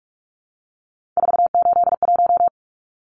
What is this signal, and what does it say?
Morse code, heard as a keyed tone, about 44 wpm, 700 Hz. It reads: 481
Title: Numbers - 3-Digit 45wpm